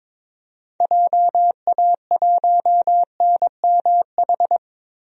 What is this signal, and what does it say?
Morse code, heard as a keyed tone, about 22 wpm, 700 Hz. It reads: JA1NMH
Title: DXCC-100 Call Signs - 6 Characters - 22wpm